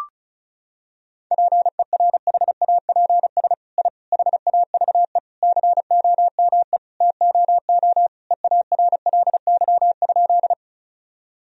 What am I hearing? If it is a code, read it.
Morse code, heard as a keyed tone, about 34 wpm, 700 Hz. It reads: PERHAPS I HAVE COME TOO EARLY?
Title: Sentences from Top 1000 Words - Review Part 2 - 35wpm